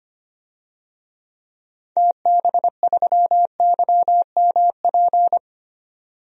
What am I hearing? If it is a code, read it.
Morse code, heard as a keyed tone, about 25 wpm, 700 Hz. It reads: TB3YMP
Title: DXCC-100 Call Signs - 6 Characters - 25wpm